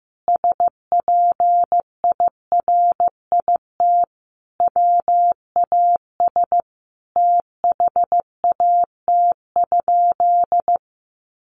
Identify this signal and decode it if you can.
Morse code, heard as a keyed tone, about 15 wpm, 700 Hz. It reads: SPIRIT WAS THAT?
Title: Sentences from Top 1400 Words 15wpm